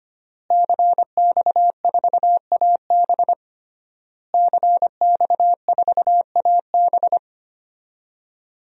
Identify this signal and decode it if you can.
Morse code, heard as a keyed tone, about 25 wpm, 700 Hz. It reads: CX4AB CX4AB
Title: DXCC-100 Call Signs - 5 Characters - Repeated 3x - 25wpm